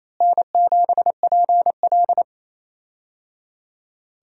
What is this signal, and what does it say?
Morse code, heard as a keyed tone, about 28 wpm, 700 Hz. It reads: N7PL
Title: USA Call Signs - 4 Characters - Repeated 3x - 28wpm